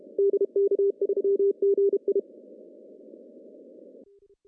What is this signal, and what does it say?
Morse code, heard as a keyed tone, about 27 wpm, 400 Hz. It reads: DK3GI
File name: CW / cqww-cw-k5tr